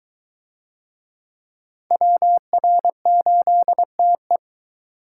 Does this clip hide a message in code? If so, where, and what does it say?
Yes: Morse code from 1.9 to 4.4 s, heard as a keyed tone, about 20 wpm, 700 Hz. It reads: WR8TE